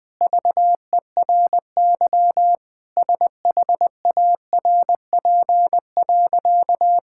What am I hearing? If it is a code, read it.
Morse code, heard as a keyed tone, about 21 wpm, 700 Hz. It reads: VERY SHARP.
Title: Sentences from Top 1600 Words 20wpm